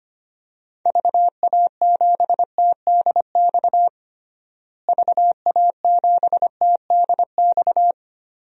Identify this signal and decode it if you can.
Morse code, heard as a keyed tone, about 25 wpm, 700 Hz. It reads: VA7TDX VA7TDX
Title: DXCC-100 Call Signs - 6 Characters - Repeated 2x - 25wpm